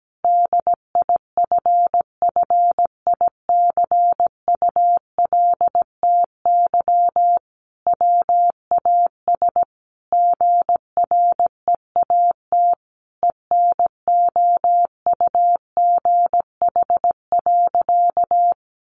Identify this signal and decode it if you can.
Morse code, heard as a keyed tone, about 17 wpm, 700 Hz. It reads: DIFFICULTY WAS GREAT ENOUGH.
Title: Sentences from Top 1500 Words 17wpm